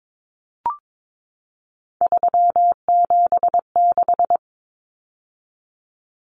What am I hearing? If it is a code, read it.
Morse code, heard as a keyed tone, about 22 wpm, 700 Hz. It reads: 376
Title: Common Cut Numbers - 3-Digit 22wpm